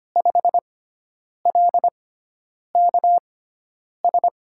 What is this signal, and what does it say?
Morse code, heard as a keyed tone, about 25 wpm, spaced as for 9 wpm, 700 Hz. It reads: 5LKS
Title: USA Call Signs - 5 Characters - 6x Character Spacing - 25wpm